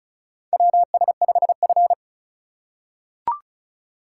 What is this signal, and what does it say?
Morse code, heard as a keyed tone, about 36 wpm, 690 Hz. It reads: WS5F
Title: USA Call Signs - 4 Characters - 2x Character Spacing - 35wpm